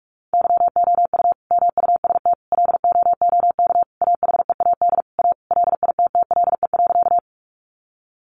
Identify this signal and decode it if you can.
Morse code, heard as a keyed tone, about 44 wpm, 700 Hz. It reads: YOU MUST LOOK AHEAD A LITTLE.
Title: Sentences from Top 1100 Words 45wpm